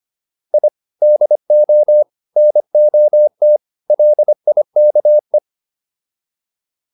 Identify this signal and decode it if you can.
Morse code, heard as a keyed tone, about 25 wpm, 600 Hz. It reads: I DO NOT LIKE